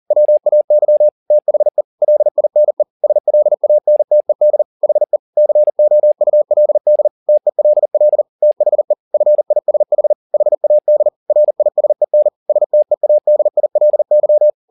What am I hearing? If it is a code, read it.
Morse code, heard as a keyed tone, about 40 wpm, 600 Hz. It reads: WAY THE LINE SLANTED HE COULD TELL THE FISH HAD RISEN STEADILY
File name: the old man and the sea 2 40wpm